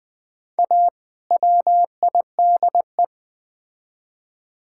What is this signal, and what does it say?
Morse code, heard as a keyed tone, about 20 wpm, 690 Hz. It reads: A WIDE
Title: Build A Phrase - 4-Word Phrases - 20wpm